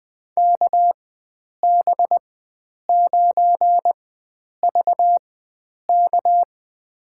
Morse code, heard as a keyed tone, about 20 wpm, spaced as for 9 wpm, 700 Hz. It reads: KB9VK
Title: USA Call Signs - 4x Character Spacing - 20wpm